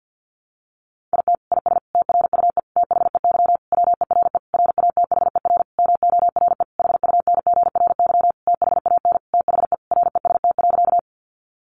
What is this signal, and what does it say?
Morse code, heard as a keyed tone, about 50 wpm, 700 Hz. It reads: IT IS TRUE THEY WERE RATHER MORE HUNGRY THAN THE REST.
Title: Sentences from Top 1400 Words 50wpm